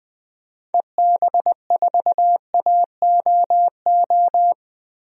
Morse code, heard as a keyed tone, about 20 wpm, 700 Hz. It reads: EB4AOO
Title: DXCC-100 Call Signs - 6 Characters - Repeated 3x - 20wpm